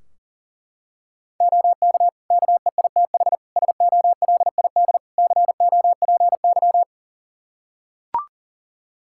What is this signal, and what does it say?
Morse code, heard as a keyed tone, about 37 wpm, 700 Hz. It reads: OK KEITH SOLID COPY
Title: Practice QSO - Exchange 3 - More Info - 40wpm